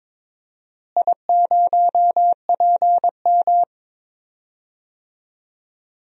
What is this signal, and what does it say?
Morse code, heard as a keyed tone, about 22 wpm, 700 Hz. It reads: I0PM
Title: DXCC-100 Call Signs - 4 Characters - Repeated 3x - 22wpm